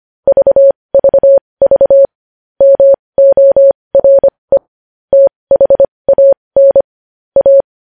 Morse code, heard as a keyed tone, about 25 wpm, spaced as for 20 wpm, 550 Hz. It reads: VVV MORE THAN A
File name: podcast / News headlines-1x-25x20